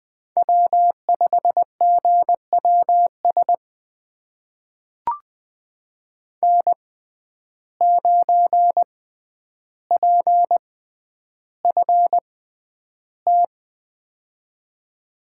Morse code, heard as a keyed tone, about 20 wpm, spaced as for 10 wpm, 700 Hz. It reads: W5GWS N9PFT
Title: USA Call Signs - 5 Characters - 6x Character Spacing - 20wpm